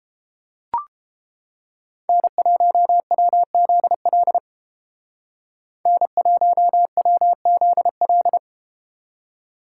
Morse code, heard as a keyed tone, about 29 wpm, 700 Hz. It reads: N1WZL N1WZL